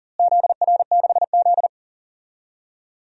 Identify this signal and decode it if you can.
Morse code, heard as a keyed tone, about 40 wpm, 700 Hz. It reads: ZR6Z
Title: DXCC-100 Call Signs - 4 Characters - Repeated 2x - 40wpm